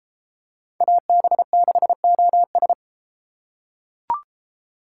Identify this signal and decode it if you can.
Morse code, heard as a keyed tone, about 32 wpm, 700 Hz. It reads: AB6OS